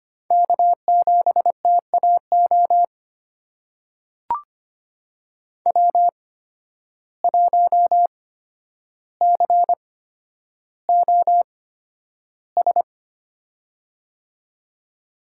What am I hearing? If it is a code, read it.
Morse code, heard as a keyed tone, about 25 wpm, spaced as for 11 wpm, 700 Hz. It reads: K7TAO W1COS
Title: USA Call Signs - 5 Characters - 8x Character Spacing - 25wpm